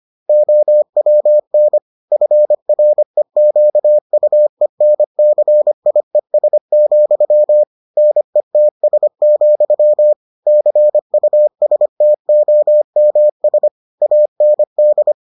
Morse code, heard as a keyed tone, about 25 wpm, 600 Hz. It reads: OWN FREQUENCIES, NETS, CUSTOMS AND